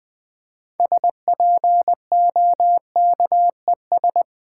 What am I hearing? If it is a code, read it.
Morse code, heard as a keyed tone, about 20 wpm, 700 Hz. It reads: SPOKES
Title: Build A Word - 6 Letters At A Time - 12-Letter Words - 20wpm